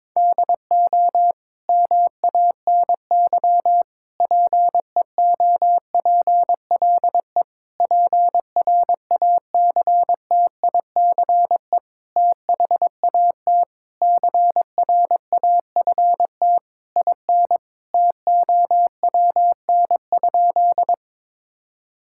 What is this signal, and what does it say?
Morse code, heard as a keyed tone, about 22 wpm, 700 Hz. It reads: DO MANY PEOPLE PRACTICE THAT CRAFT IN TOWN?
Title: Sentences from Top 1000 Words - Review Part 6 - 22wpm